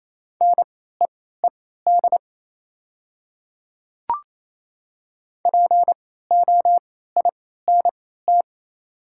Morse code, heard as a keyed tone, about 28 wpm, spaced as for 16 wpm, 700 Hz. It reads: NEED POINT
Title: 3x Character Spacing - Top 200 Words - 28wpm